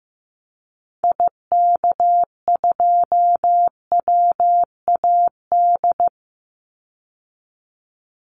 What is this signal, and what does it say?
Morse code, heard as a keyed tone, about 15 wpm, 700 Hz. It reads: IK2WAD
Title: Call Signs Repeated 3x - 15wpm